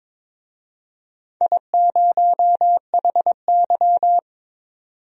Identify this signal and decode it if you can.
Morse code, heard as a keyed tone, about 22 wpm, 700 Hz. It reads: I0HY
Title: DXCC-100 Call Signs - 4 Characters - 22wpm